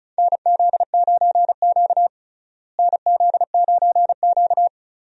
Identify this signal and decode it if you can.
Morse code, heard as a keyed tone, about 36 wpm, 700 Hz. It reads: NZ9Q NZ9Q
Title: USA Call Signs - 4 Characters - Repeated 3x - 35wpm